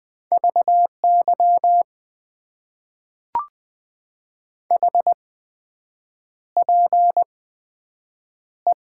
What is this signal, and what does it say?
Morse code, heard as a keyed tone, about 20 wpm, spaced as for 7 wpm, 700 Hz. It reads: VY HPE
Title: Top 90 QSO Elements - 8x Character Spacing - 20wpm